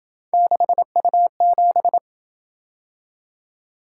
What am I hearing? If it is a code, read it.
Morse code, heard as a keyed tone, about 28 wpm, 700 Hz. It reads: 6U7